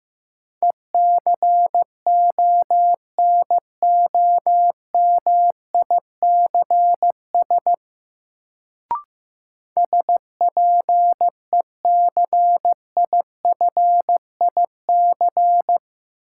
Morse code, heard as a keyed tone, about 16 wpm, 700 Hz. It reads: ECONOMICS SPECIFIC
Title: Sentences from Top 1000 Words - Review Part 3 - New Word Drill - 15wpm